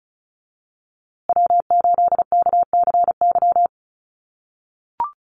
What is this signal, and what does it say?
Morse code, heard as a keyed tone, about 34 wpm, 700 Hz. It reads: W8KCY
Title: USA Call Signs Repeated 3x 35wpm